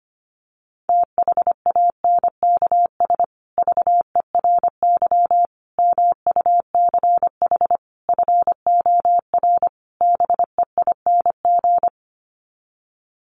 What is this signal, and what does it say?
Morse code, heard as a keyed tone, about 25 wpm, 700 Hz. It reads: THANKS VERY MUCH FOR BEING
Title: Sets of 5 Words Encore 25wpm